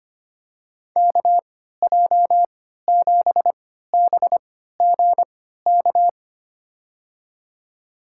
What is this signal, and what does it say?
Morse code, heard as a keyed tone, about 25 wpm, spaced as for 14 wpm, 700 Hz. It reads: KJ7BGK
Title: USA Call Signs - 6 Characters - 3x Character Spacing - 25wpm